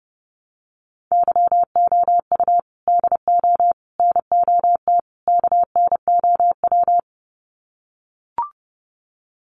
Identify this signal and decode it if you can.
Morse code, heard as a keyed tone, about 30 wpm, 700 Hz. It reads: YOU DO NOT KNOW
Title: Sets of 4 Words 30wpm